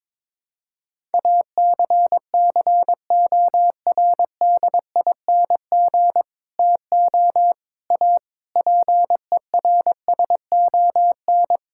Morse code, heard as a keyed tone, about 21 wpm, 700 Hz. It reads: ACCORDING TO A PERSON
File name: practice / 22wpm / Sets of 8 Words - 22wpm